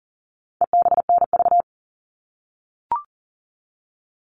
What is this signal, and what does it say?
Morse code, heard as a keyed tone, about 40 wpm, 700 Hz. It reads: EBNV
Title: All Cut Numbers - 4-Digit 40wpm